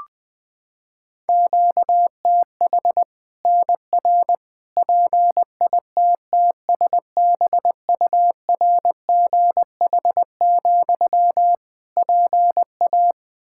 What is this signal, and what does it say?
Morse code, heard as a keyed tone, about 21 wpm, 700 Hz. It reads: QTH NR PITTSBURGH, PA
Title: Practice QSO - Exchange 1 - Beginning - 20wpm